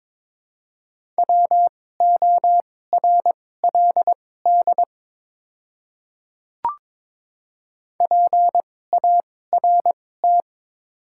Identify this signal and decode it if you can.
Morse code, heard as a keyed tone, about 22 wpm, spaced as for 16 wpm, 700 Hz. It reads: WORLD PART
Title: 2x Character Spacing - Top 200 Words - 22wpm